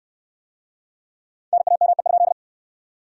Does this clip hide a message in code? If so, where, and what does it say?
Yes: Morse code from 1.5 to 2.3 s, heard as a keyed tone, about 66 wpm, 700 Hz. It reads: NAME?